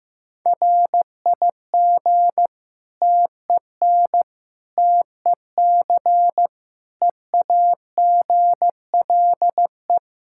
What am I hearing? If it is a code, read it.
Morse code, heard as a keyed tone, about 15 wpm, 700 Hz. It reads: RIG TEN TEC EAGLE
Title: Practice QSO - Exchange 2 - Info - 15wpm